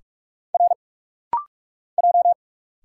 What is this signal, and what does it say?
Morse code, heard as a keyed tone, about 46 wpm, 700 Hz. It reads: R J
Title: Single Letters - Rapid-Fire 45wpm